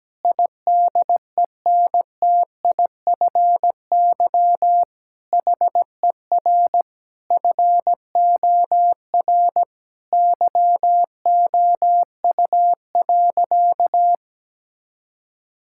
Morse code, heard as a keyed tone, about 17 wpm, 700 Hz. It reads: IDENTIFY HER FOR YOU.
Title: Sentences from Top 1000 Words - Review Part 6 - 17wpm